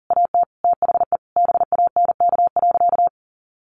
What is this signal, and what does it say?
Morse code, heard as a keyed tone, about 37 wpm, 700 Hz. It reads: AT THE BANK.
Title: Sentences from Top 1400 Words 40wpm